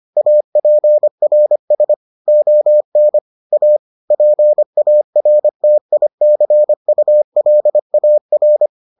Morse code, heard as a keyed tone, about 25 wpm, 600 Hz. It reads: APRS ON A PARTICULAR